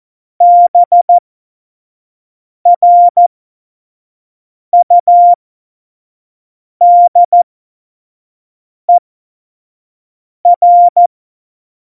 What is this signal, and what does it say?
Morse code, heard as a keyed tone, about 14 wpm, spaced as for 5 wpm, 700 Hz. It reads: BRUDER
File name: Morsekurs / Witze / Witz 04 August 2020, 25 F70